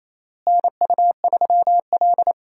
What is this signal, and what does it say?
Morse code, heard as a keyed tone, about 27 wpm, 700 Hz. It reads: NU3L
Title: USA Call Signs - 4 Characters - Repeated 3x - 28wpm